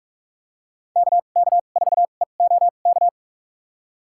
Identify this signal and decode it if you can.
Morse code, heard as a keyed tone, about 46 wpm, spaced as for 33 wpm, 690 Hz. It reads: KK4EOK
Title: USA Call Signs - 6 Characters - 2x Character Spacing - 45wpm